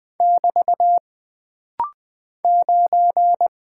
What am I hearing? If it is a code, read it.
Morse code, heard as a keyed tone, about 20 wpm, 700 Hz. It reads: <BT> 9
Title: Single Character (Letter:Number:Punctuation:Prosign) - Rapid-Fire - 20wpm